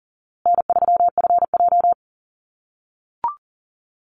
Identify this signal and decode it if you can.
Morse code, heard as a keyed tone, about 40 wpm, 700 Hz. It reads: N3FJ
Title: USA Call Signs Repeated 3x 40wpm